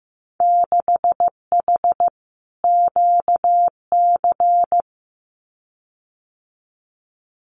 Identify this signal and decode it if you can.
Morse code, heard as a keyed tone, about 15 wpm, 700 Hz. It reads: 6H QC
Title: Winter Field Day Exchanges Encore 15wpm